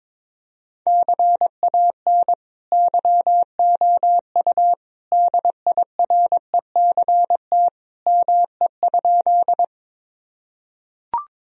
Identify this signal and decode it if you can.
Morse code, heard as a keyed tone, about 21 wpm, 700 Hz. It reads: CAN YOU DIRECT ME?
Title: Sentences from Top 1000 Words - Review Part 2 - 22wpm